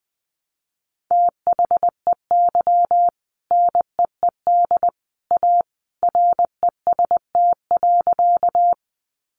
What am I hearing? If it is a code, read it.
Morse code, heard as a keyed tone, about 20 wpm, 700 Hz. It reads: THEY NEED A REST.
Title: Sentences from Top 500 Words 20wpm